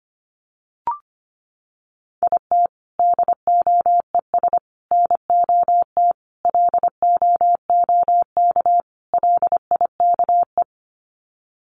Morse code, heard as a keyed tone, about 25 wpm, 700 Hz. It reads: IT DOES NOT LOOK LIKE
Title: Sets of 5 Words Encore2 25wpm